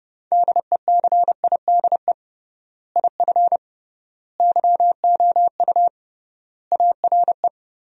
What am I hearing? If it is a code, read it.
Morse code, heard as a keyed tone, about 30 wpm, 700 Hz. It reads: DECIDE IF YOU ARE
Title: 3x Word-Spacing - Sentences from Top 500 Words - 30wpm